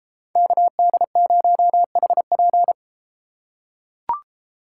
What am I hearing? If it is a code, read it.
Morse code, heard as a keyed tone, about 33 wpm, 700 Hz. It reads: KD0HP